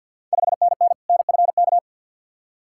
Morse code, heard as a keyed tone, about 53 wpm, 690 Hz. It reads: 5NN NFK
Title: Ontario QSO Party Exchange 50wpm